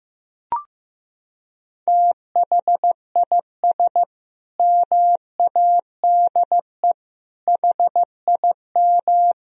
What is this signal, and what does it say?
Morse code, heard as a keyed tone, about 16 wpm, 700 Hz. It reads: THIS MADE HIM
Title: Sentences from Top 500 Words and Abbreviations 15wpm